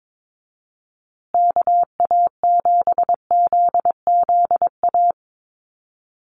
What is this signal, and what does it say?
Morse code, heard as a keyed tone, about 22 wpm, 700 Hz. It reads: KA7ZZA
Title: USA Call Signs Repeated 3x 22wpm